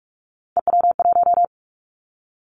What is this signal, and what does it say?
Morse code, heard as a keyed tone, about 44 wpm, 700 Hz. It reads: EW1
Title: Callsign Prefixes for Top 50 DXCC - 45wpm